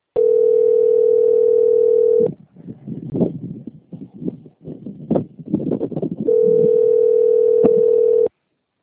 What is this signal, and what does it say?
Ringback tone (North America), 440 + 480 Hz.